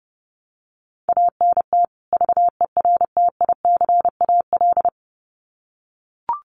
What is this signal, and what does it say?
Morse code, heard as a keyed tone, about 30 wpm, 700 Hz. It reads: ANT VERTICAL
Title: Common QSO Phrases 30wpm